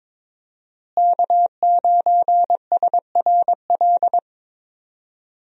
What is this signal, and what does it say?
Morse code, heard as a keyed tone, about 21 wpm, 700 Hz. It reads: K9SRL
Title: USA Call Signs - 5 Characters - Repeated 2x - 22wpm